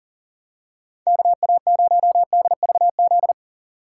Morse code, heard as a keyed tone, about 38 wpm, 700 Hz. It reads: KA0DVZ
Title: USA Call Signs - 6 Characters - 4x Character Spacing - 40wpm